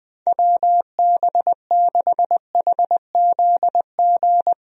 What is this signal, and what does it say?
Morse code, heard as a keyed tone, about 20 wpm, 700 Hz. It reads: WB6HZG
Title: USA Call Signs - 6 Characters - Repeated 3x - 20wpm